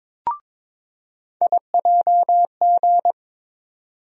Morse code, heard as a keyed tone, about 21 wpm, 700 Hz. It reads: IJG
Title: USA Call Sign Postfixes - 3 Letters 22wpm